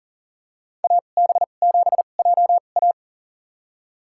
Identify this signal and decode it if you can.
Morse code, heard as a keyed tone, about 40 wpm, spaced as for 29 wpm, 700 Hz. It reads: AB7JA
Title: USA Call Signs - 5 Characters - 2x Character Spacing - 40wpm